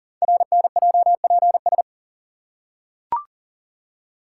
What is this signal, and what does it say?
Morse code, heard as a keyed tone, about 38 wpm, 690 Hz. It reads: RNJPS
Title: Code Groups - 5 Letters - 40wpm